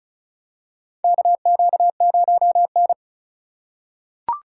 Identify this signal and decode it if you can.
Morse code, heard as a keyed tone, about 36 wpm, 700 Hz. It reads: KQ0N
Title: USA Call Signs - 4 Characters - 2x Character Spacing - 35wpm